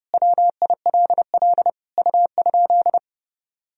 Morse code, heard as a keyed tone, about 30 wpm, 700 Hz. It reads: WILL U?
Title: Sentences from Top 300 Words and Abbreviations 30wpm